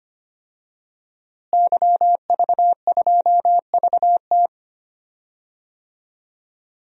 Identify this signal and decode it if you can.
Morse code, heard as a keyed tone, about 25 wpm, 700 Hz. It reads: YV2VT
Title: DXCC-100 Call Signs - 5 Characters - Repeated 3x - 25wpm